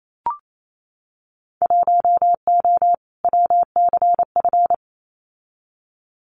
Morse code, heard as a keyed tone, about 28 wpm, 700 Hz. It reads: 1O WCF
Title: Winter Field Day Exchanges 28wpm